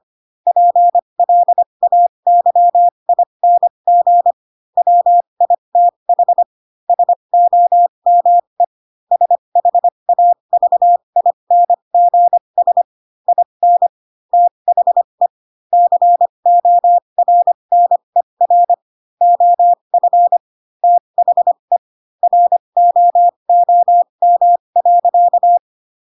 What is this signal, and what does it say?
Morse code, heard as a keyed tone, about 25 wpm, spaced as for 22 wpm, 700 Hz. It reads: PLAYING WITH SOME SHAVINGS IN THE CORNER OF THE ROOM.
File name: Pinocchio / Pinocchio ch7 18 25 ik1hge